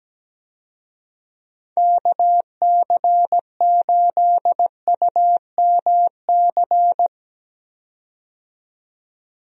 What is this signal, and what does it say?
Morse code, heard as a keyed tone, about 17 wpm, 700 Hz. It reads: KC8UMC
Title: USA Call Signs - 6 Characters - 17wpm